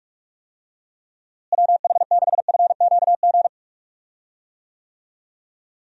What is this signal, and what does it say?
Morse code, heard as a keyed tone, about 46 wpm, 700 Hz. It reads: WH6FQG
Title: USA Call Signs - 6 Characters - Repeated 3x - 45wpm